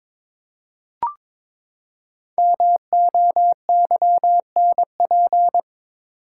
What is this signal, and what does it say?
Morse code, heard as a keyed tone, about 22 wpm, 700 Hz. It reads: MOYNP
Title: Code Groups - 5 Letter:Numbers - 22wpm